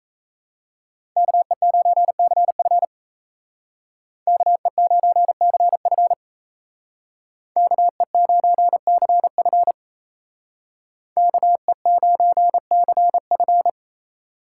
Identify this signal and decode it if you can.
Morse code, heard as a keyed tone, about 36 wpm, 700 Hz. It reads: KE9CF KE9CF KE9CF KE9CF